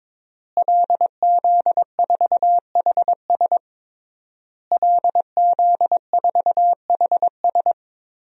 Morse code, heard as a keyed tone, about 22 wpm, 700 Hz. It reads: LZ4HS LZ4HS
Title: DXCC-100 Call Signs - 5 Characters - Repeated 3x - 22wpm